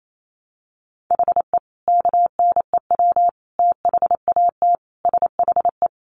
Morse code, heard as a keyed tone, about 28 wpm, 700 Hz. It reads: HE KNEW THAT SHE
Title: Sentences from Top 1100 Words 28wpm